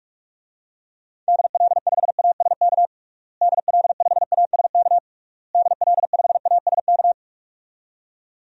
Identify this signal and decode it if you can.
Morse code, heard as a keyed tone, about 46 wpm, 700 Hz. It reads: DL5ASK DL5ASK DL5ASK
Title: DXCC-100 Call Signs - 6 Characters - Repeated 3x - 45wpm